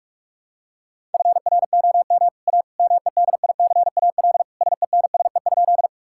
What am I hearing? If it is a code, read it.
Morse code, heard as a keyed tone, about 47 wpm, 700 Hz. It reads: FROM A MEDICAL SENSE?
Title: Sentences from Top 1000 Words - Review Part 5 - 45wpm